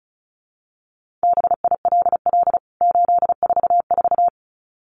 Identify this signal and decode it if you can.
Morse code, heard as a keyed tone, about 34 wpm, 700 Hz. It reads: BILL 844
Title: CWT Exchange 35wpm